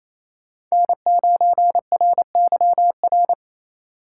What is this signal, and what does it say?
Morse code, heard as a keyed tone, about 28 wpm, 700 Hz. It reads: N9RYR
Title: USA Call Signs - 5 Characters - 6x Character Spacing - 28wpm